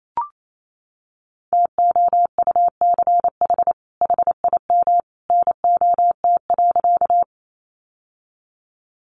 Morse code, heard as a keyed tone, about 28 wpm, 700 Hz. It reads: TOUCH HIM NOT.
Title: Sentences from Top 1200 Words 28wpm